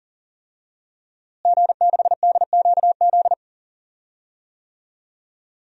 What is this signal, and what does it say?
Morse code, heard as a keyed tone, about 37 wpm, 700 Hz. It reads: G6DQZ
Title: DXCC-100 Call Signs - 5 Characters - Repeated 2x - 40wpm